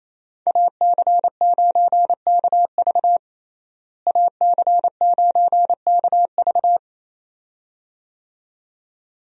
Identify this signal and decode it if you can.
Morse code, heard as a keyed tone, about 27 wpm, 700 Hz. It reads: AC9KV AC9KV
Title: USA Call Signs - 5 Characters - Repeated 2x - 28wpm